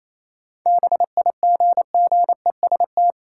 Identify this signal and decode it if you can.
Morse code, heard as a keyed tone, about 28 wpm, 700 Hz. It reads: BIGGEST
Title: Sets of 4 Words - New Word Drill - 28wpm